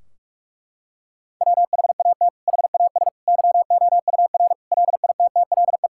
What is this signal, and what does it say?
Morse code, heard as a keyed tone, about 41 wpm, 700 Hz. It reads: WHAT HAS YOUR LITTLE
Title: Sentences from Top 1600 Words 45wpm